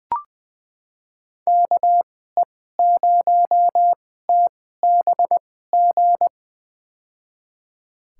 Morse code, heard as a keyed tone, about 20 wpm, spaced as for 14 wpm, 700 Hz. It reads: KE0TBG
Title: USA Call Signs - 2x Character Spacing - 20wpm